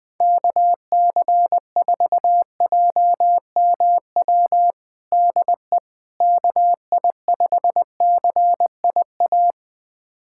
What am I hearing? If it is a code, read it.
Morse code, heard as a keyed tone, about 21 wpm, 700 Hz. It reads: KC4JMW DE KI5CIA
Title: USA Callsign DE Callsign 20wpm